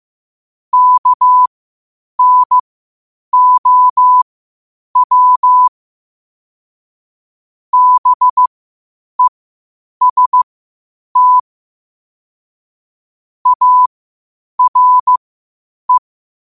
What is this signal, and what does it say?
Morse code, heard as a keyed tone, about 15 wpm, spaced as for 8 wpm, 1000 Hz. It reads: KNOW BEST ARE
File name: qotdfiles / 8wpm